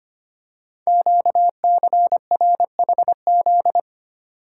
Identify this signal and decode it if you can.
Morse code, heard as a keyed tone, about 25 wpm, 700 Hz. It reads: QCRHZ
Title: Code Groups - 5 Letters - 25wpm